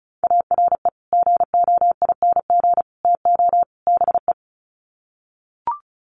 Morse code, heard as a keyed tone, about 34 wpm, 700 Hz. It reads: ARE GOING TO BE
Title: Sets of 4 Words 35wpm